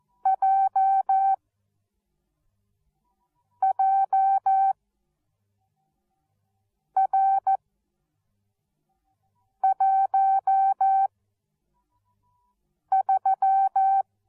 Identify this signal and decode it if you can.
Morse code, heard as a keyed tone, about 14 wpm, spaced as for 4 wpm, 770 Hz. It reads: JJR13
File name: Onlinelehrgang / morsen / DJ4UF Morsen13